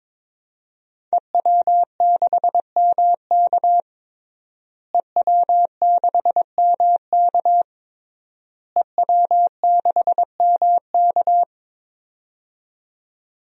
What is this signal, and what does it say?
Morse code, heard as a keyed tone, about 22 wpm, 700 Hz. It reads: EW6MK EW6MK EW6MK
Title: DXCC-100 Call Signs - 5 Characters - Repeated 3x - 22wpm